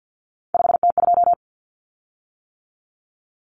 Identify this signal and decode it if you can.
Morse code, heard as a keyed tone, about 50 wpm, 700 Hz. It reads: 5T2
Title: Common Cut Numbers - 3-Digit 50wpm